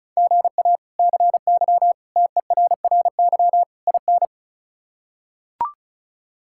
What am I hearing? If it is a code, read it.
Morse code, heard as a keyed tone, about 34 wpm, 700 Hz. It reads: GA CY TERRY IN
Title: K1USN SST QSO Practice - 35wpm